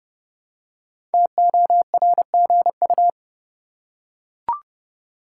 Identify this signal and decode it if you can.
Morse code, heard as a keyed tone, about 31 wpm, 700 Hz. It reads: TORGU
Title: Code Groups - 5 Letters - 30wpm